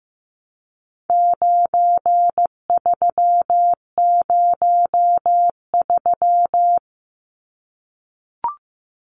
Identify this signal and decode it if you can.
Morse code, heard as a keyed tone, about 14 wpm, 700 Hz. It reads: 9303
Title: Numbers - 4-Digit 15wpm